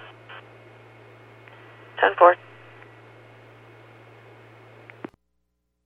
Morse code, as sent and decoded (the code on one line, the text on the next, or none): none